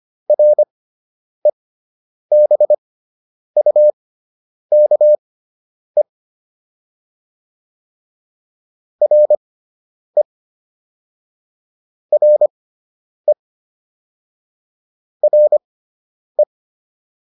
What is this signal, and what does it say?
Morse code, heard as a keyed tone, about 25 wpm, spaced as for 9 wpm, 600 Hz. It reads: REBUKE RE RE RE